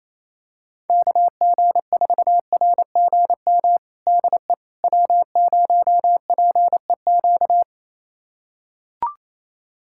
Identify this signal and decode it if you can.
Morse code, heard as a keyed tone, about 28 wpm, 700 Hz. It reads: KG4RGM DE W0PEQ
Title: USA Callsign DE Callsign 28wpm